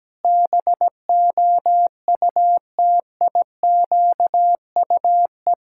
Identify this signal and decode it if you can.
Morse code, heard as a keyed tone, about 17 wpm, 700 Hz. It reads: BOUTIQUE
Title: Pangram Challenge v2 - 17wpm